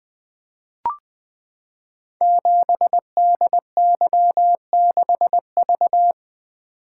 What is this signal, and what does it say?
Morse code, heard as a keyed tone, about 20 wpm, 700 Hz. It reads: 7DY6V
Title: Code Groups - 5 Letter:Numbers - 20wpm